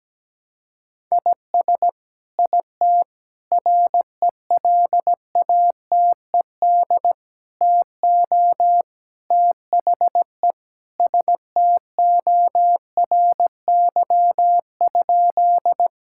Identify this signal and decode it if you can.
Morse code, heard as a keyed tone, about 17 wpm, 700 Hz. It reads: IS IT RELATED TO THE STORY?
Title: Sentences from Top 1000 Words - Review Part 5 - 17wpm